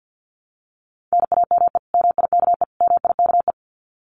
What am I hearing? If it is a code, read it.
Morse code, heard as a keyed tone, about 50 wpm, 700 Hz. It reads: NAME MIKE MIKE
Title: Common QSO Phrases Encore 50wpm